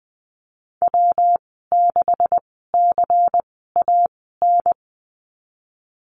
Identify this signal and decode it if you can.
Morse code, heard as a keyed tone, about 19 wpm, spaced as for 14 wpm, 700 Hz. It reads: W6CAN
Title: 128 Lesson 21g - Number 6 - Call Signs 20wpm